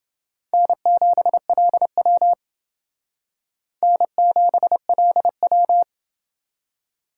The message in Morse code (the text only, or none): N7LW N7LW